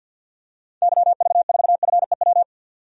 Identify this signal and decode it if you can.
Morse code, heard as a keyed tone, about 52 wpm, 700 Hz. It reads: YV4FEW
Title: DXCC-100 Call Signs - 6 Characters - 50wpm